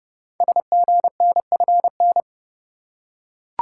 Morse code, heard as a keyed tone, about 30 wpm, 700 Hz. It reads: SGNFN
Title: Code Groups - 5 Letters - 30wpm